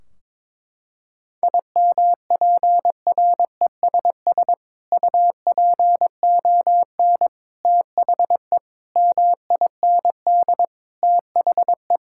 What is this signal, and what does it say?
Morse code, heard as a keyed tone, about 22 wpm, 700 Hz. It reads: IMPRESS UPON THE MIND THE
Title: Sentences from Top 1600 Words 22wpm